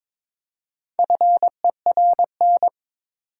Morse code, heard as a keyed tone, about 22 wpm, 700 Hz. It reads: FERN
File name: Difficult Words with Hints - 22wpm